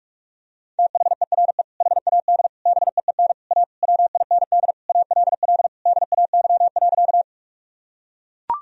Morse code, heard as a keyed tone, about 41 wpm, 700 Hz. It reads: THERE HAD BEEN A WIND ALL DAY.
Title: Sentences from Top 1000 Words - Review Part 2 - 45wpm